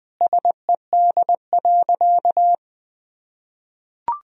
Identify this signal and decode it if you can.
Morse code, heard as a keyed tone, about 21 wpm, 700 Hz. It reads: SED.
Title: Sentences from Top 100 Words and Abbreviations 20wpm